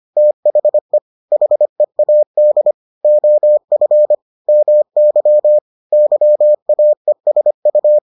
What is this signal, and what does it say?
Morse code, heard as a keyed tone, about 25 wpm, 600 Hz. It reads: THE HEAD OF MY YAESU